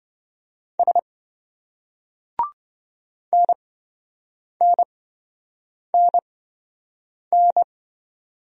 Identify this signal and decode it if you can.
Morse code, heard as a keyed tone, about 26 wpm, spaced as for 7 wpm, 700 Hz. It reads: S NNNN